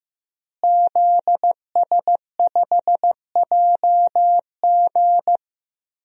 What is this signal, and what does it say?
Morse code, heard as a keyed tone, about 15 wpm, 700 Hz. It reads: ZS5JG
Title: DXCC-100 Call Signs - 5 Characters - Repeated 3x - 15wpm